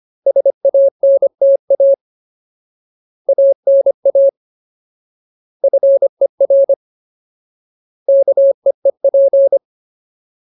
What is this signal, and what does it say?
Morse code, heard as a keyed tone, about 25 wpm, 550 Hz. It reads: SANTA ANA FER KEEP
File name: Ultimate Word List / 25WPM Ultimate